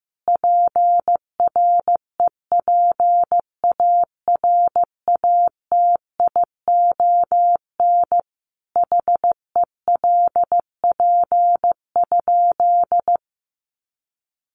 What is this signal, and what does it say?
Morse code, heard as a keyed tone, about 15 wpm, 700 Hz. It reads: PREPARATION HELP?
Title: Sentences from Top 1400 Words 15wpm